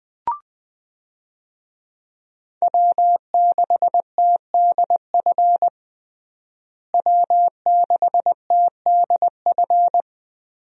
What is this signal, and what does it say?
Morse code, heard as a keyed tone, about 21 wpm, 700 Hz. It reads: W6TDF W6TDF
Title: USA Call Signs - 5 Characters - Repeated 3x - 20wpm